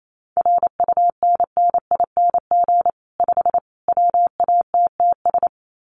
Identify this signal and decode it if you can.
Morse code, heard as a keyed tone, about 28 wpm, 700 Hz. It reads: RUNNING 5 WATTS
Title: Common QSO Phrases 28wpm